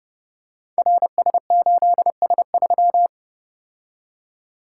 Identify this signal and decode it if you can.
Morse code, heard as a keyed tone, about 30 wpm, 700 Hz. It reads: RS8S3
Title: Code Groups - 5 Letter:Numbers - 30wpm